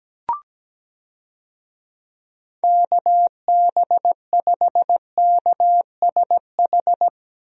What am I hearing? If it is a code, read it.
Morse code, heard as a keyed tone, about 17 wpm, 700 Hz. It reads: KB5KSH
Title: USA Call Signs - 6 Characters - Repeated 2x - 17wpm